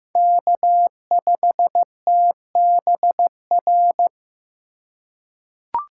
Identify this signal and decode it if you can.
Morse code, heard as a keyed tone, about 15 wpm, 700 Hz. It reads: K5TBR
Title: USA Call Signs - 5 Characters - 2x Character Spacing - 15wpm